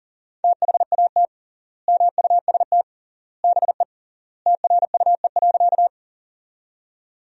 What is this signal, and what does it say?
Morse code, heard as a keyed tone, about 40 wpm, 700 Hz. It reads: THAT MUST BE TRUE.
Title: 3x Word-Spacing - Sentences from Top 400 Words - 40wpm